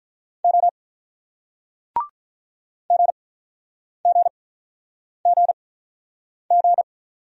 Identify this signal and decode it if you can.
Morse code, heard as a keyed tone, about 44 wpm, 700 Hz. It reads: O G G G G